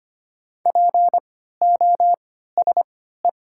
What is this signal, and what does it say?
Morse code, heard as a keyed tone, about 25 wpm, spaced as for 14 wpm, 690 Hz. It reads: POSE
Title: 3x Character Spacing - Top 500 Words - 25wpm